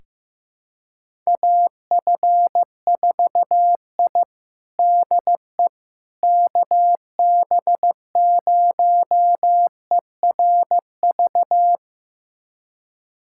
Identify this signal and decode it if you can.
Morse code, heard as a keyed tone, about 15 wpm, 700 Hz. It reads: AF4I DE KB0ERV
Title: USA Callsign DE Callsign 15wpm